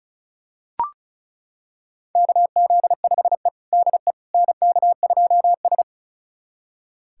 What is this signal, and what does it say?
Morse code, heard as a keyed tone, about 36 wpm, 700 Hz. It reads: KZ5E DE NK2S
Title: USA Callsign DE Callsign - 4 Characters - 35wpm